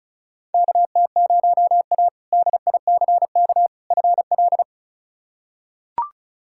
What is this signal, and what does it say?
Morse code, heard as a keyed tone, about 36 wpm, 700 Hz. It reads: KT0A DICK FL
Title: K1USN SST QSO Practice - 35wpm